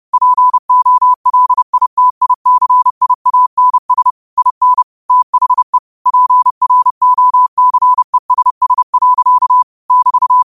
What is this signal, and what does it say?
Morse code, heard as a keyed tone, about 30 wpm, 1000 Hz. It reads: POLITICIANS IN THE PROCESS. <BT>
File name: qotdfiles / 30wpm